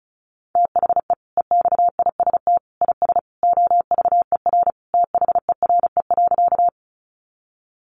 Morse code, heard as a keyed tone, about 34 wpm, 700 Hz. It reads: THE EXIST IS OVER THERE.
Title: Sentences from Top 1100 Words 35wpm